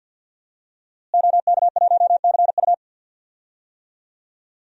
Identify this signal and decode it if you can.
Morse code, heard as a keyed tone, about 50 wpm, 700 Hz. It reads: OK1CU
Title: DXCC-100 Call Signs - 5 Characters - Repeated 2x - 50wpm